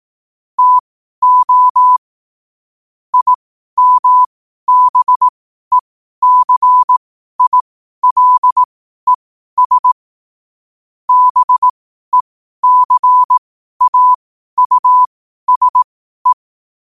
Morse code, heard as a keyed tone, about 18 wpm, spaced as for 12 wpm, 1000 Hz. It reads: TO IMBECILES BECAUSE
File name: 12wpm